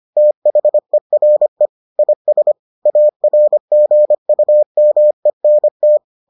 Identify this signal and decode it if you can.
Morse code, heard as a keyed tone, about 25 wpm, 600 Hz. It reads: THERE IS ARGUMENT